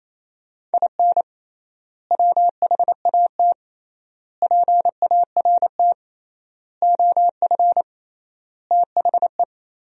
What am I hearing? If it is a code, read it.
Morse code, heard as a keyed tone, about 28 wpm, 700 Hz. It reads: IN WHAT PART OF THE
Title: 3x Word-Spacing - Sentences from Top 200 Words - 28wpm